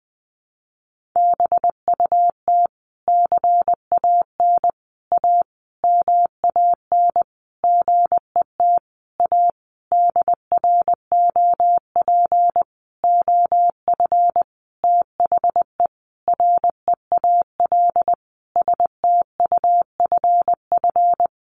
Morse code, heard as a keyed tone, about 20 wpm, 700 Hz. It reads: BUT CAN A MAN GET A DROP OF THE REAL STUFF
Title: Sentences from Top 1400 Words 20wpm